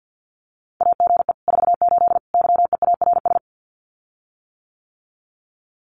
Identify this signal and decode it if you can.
Morse code, heard as a keyed tone, about 48 wpm, 700 Hz. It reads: AGE 48 YEARS
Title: Common QSO Phrases Encore 50wpm